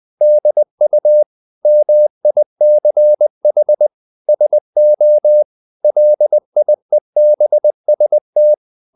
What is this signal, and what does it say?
Morse code, heard as a keyed tone, about 20 wpm, 600 Hz. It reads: DU MICH SO LIEBST
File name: dk5ke / Humor-100BpM